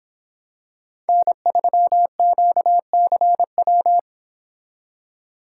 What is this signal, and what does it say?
Morse code, heard as a keyed tone, about 26 wpm, 700 Hz. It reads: N3QCW